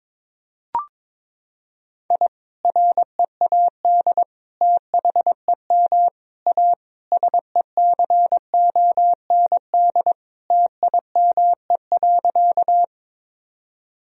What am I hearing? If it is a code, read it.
Morse code, heard as a keyed tone, about 22 wpm, 700 Hz. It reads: I READ THEM A SECOND TIME.
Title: Sentences from Top 1000 Words - Review Part 1 - 22wpm